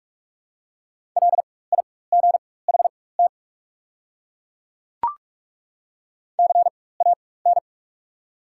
Morse code, heard as a keyed tone, about 42 wpm, spaced as for 20 wpm, 690 Hz. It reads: LIGHT CAN
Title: 4x Character Spacing - Top 300 Words - 45wpm